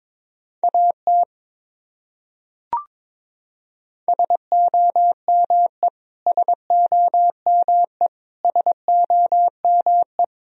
Morse code, heard as a keyed tone, about 21 wpm, 700 Hz. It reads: AT SOME SOME SOME
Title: Top 100 Words Repeated 22wpm